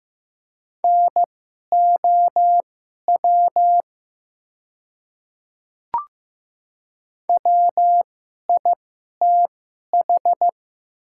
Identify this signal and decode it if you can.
Morse code, heard as a keyed tone, about 15 wpm, spaced as for 11 wpm, 700 Hz. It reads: NOW WITH
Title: 2x Character Spacing - Top 100 Words - 15wpm